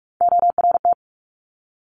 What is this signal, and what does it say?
Morse code, heard as a keyed tone, about 46 wpm, 700 Hz. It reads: ORT
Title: Top Three Letter Combinations in Words 45wpm